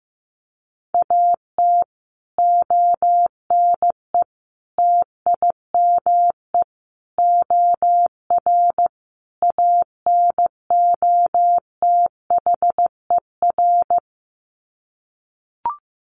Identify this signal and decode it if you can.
Morse code, heard as a keyed tone, about 15 wpm, 700 Hz. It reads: AT ONE TIME OR ANOTHER
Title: Sets of 5 Words Encore 15wpm